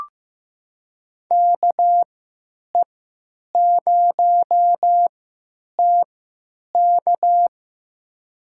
Morse code, heard as a keyed tone, about 15 wpm, spaced as for 9 wpm, 700 Hz. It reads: KE0TK
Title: USA Call Signs - 5 Characters - 3x Character Spacing - 15wpm